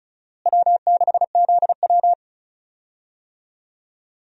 Morse code, heard as a keyed tone, about 36 wpm, 700 Hz. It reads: W6ZW
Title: USA Call Signs - 4 Characters - Repeated 2x - 35wpm